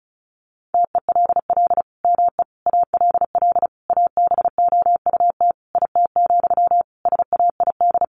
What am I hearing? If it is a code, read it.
Morse code, heard as a keyed tone, about 34 wpm, 700 Hz. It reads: TELL ME ALL ABOUT IT, SAID
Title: Sentences from Top 1500 Words 35wpm